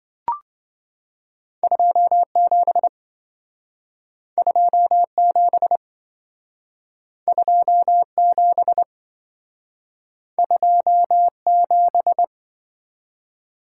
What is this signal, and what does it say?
Morse code, heard as a keyed tone, about 27 wpm, 700 Hz. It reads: 27 27 27 27